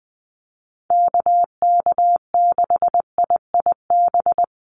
Morse code, heard as a keyed tone, about 20 wpm, 700 Hz. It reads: KK6IIB
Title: USA Call Signs Repeated 2x 20wpm